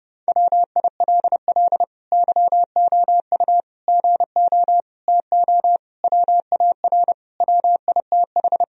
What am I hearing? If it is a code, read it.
Morse code, heard as a keyed tone, about 30 wpm, 700 Hz. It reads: WILL YOU GO TO WAR WITH
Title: Sentences from Top 1000 Words - Review Part 2 - 30wpm